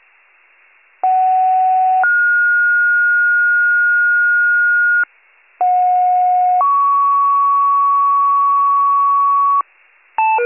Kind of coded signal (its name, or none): two-tone page